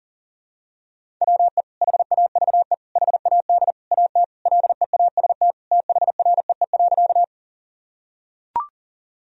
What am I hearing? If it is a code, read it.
Morse code, heard as a keyed tone, about 37 wpm, 700 Hz. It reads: WE HAVE HAD AT LEAST THREE.
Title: Sentences from Top 1000 Words - Review Part 1 - 40wpm